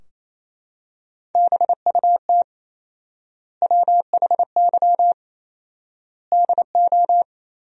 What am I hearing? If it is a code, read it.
Morse code, heard as a keyed tone, about 28 wpm, 700 Hz. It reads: BUT WHY DO
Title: 4x Word-Spacing - Sentences from Top 200 Words - 28wpm